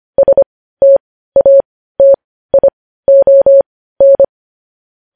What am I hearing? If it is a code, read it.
Morse code, heard as a keyed tone, about 25 wpm, spaced as for 15 wpm, 550 Hz. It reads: STATION
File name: podcast / News headlines-1x-25x15